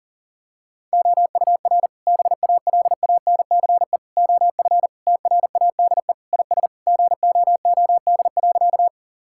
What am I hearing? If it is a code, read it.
Morse code, heard as a keyed tone, about 40 wpm, 700 Hz. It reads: OUR BALANCE OF TRADE IS GOOD.
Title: Sentences from Top 1000 Words - Review Part 6 - 40wpm